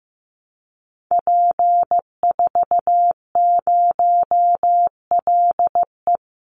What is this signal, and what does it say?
Morse code, heard as a keyed tone, about 15 wpm, 700 Hz. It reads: P40LE
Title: Call Signs Repeated 2x - 15wpm